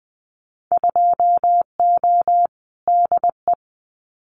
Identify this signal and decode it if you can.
Morse code, heard as a keyed tone, about 20 wpm, 700 Hz. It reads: 2O DE
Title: Winter Field Day Exchanges Encore 20wpm